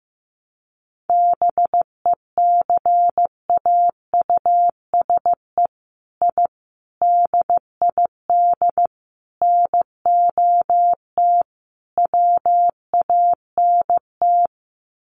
Morse code, heard as a keyed tone, about 15 wpm, 700 Hz. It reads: BECAUSE I DID NOT WANT
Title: Sets of 5 Words Encore2 15wpm